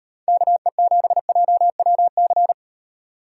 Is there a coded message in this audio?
Yes: Morse code from 0.3 to 2.5 s, heard as a keyed tone, about 38 wpm, 700 Hz. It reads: KE7JWC